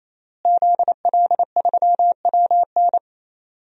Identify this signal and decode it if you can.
Morse code, heard as a keyed tone, about 27 wpm, 700 Hz. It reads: ZL3WN
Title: DXCC-100 Call Signs - 5 Characters - Repeated 3x - 28wpm